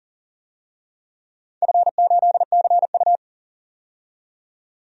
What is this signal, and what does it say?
Morse code, heard as a keyed tone, about 37 wpm, 700 Hz. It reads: F8CU
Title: DXCC-100 Call Signs - 4 Characters - 40wpm